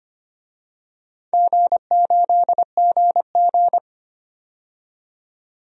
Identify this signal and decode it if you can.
Morse code, heard as a keyed tone, about 25 wpm, 700 Hz. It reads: G8GG
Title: DXCC-100 Call Signs - 4 Characters - 25wpm